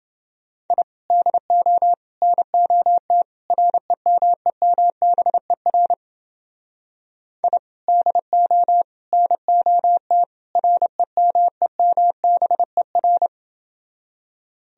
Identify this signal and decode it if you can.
Morse code, heard as a keyed tone, about 29 wpm, 700 Hz. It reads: I DO NOT REMEMBER I DO NOT REMEMBER